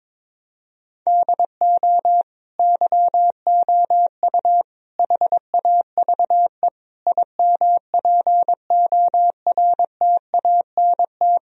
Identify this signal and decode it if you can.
Morse code, heard as a keyed tone, about 22 wpm, 700 Hz. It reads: DO YOU HAVE IMPORTANT
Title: Sentences from Top 1000 Words - Review Part 3 - 22wpm